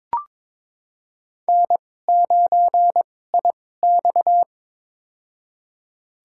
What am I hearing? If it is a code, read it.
Morse code, heard as a keyed tone, about 22 wpm, spaced as for 16 wpm, 700 Hz. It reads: N9IX
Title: USA Call Signs - 2x Character Spacing - 22wpm